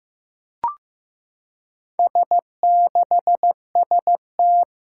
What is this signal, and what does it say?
Morse code, heard as a keyed tone, about 15 wpm, 700 Hz. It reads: S6ST
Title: Four Challenging Letter-Number Combinations 15wpm